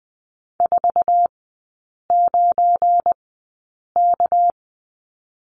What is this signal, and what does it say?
Morse code, heard as a keyed tone, about 20 wpm, spaced as for 8 wpm, 700 Hz. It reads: 49K
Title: 218 Lesson 35e - Number 0 - Three Characters 20wpm